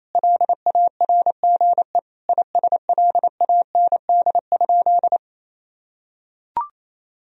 Morse code, heard as a keyed tone, about 28 wpm, 700 Hz. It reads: LARGE ISLAND?
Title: Sentences from Top 1000 Words - Review Part 3 - 28wpm